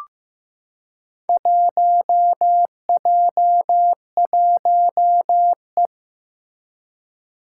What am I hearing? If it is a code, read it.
Morse code, heard as a keyed tone, about 15 wpm, 700 Hz. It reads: 1J1E
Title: Four Challenging Letter-Number Combinations 15wpm